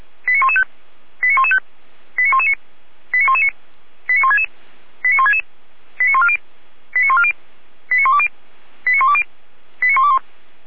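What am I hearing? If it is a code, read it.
Five-tone selective call (ZVEI), heard as quick runs of five tones, tones of about 70 ms. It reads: 89115 89115 89119 89119 89166 89166 89140 89140 89120 89120 89121
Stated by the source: Five-tone ZVEI selective call 89092 received by SDR.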